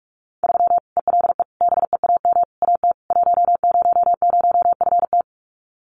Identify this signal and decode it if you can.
Morse code, heard as a keyed tone, about 44 wpm, 700 Hz. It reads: 3 ELE BEAM AT 100FT
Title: Common QSO Phrases 45wpm